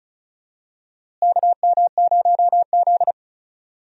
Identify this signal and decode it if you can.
Morse code, heard as a keyed tone, about 34 wpm, 700 Hz. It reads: KM0Z
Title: USA Call Signs - 4 Characters - Repeated 3x - 35wpm